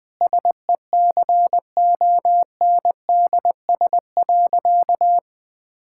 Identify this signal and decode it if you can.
Morse code, heard as a keyed tone, about 20 wpm, 700 Hz. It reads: SECONDS.
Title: All-Age Appropriate Jokes - 20wpm